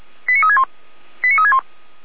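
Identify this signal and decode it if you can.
Five-tone selective call (ZVEI), heard as quick runs of five tones, tones of about 70 ms. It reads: 89361 89361
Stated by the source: ZVEI five-tone call received by SDR, from Feuerwehr Orlamünde.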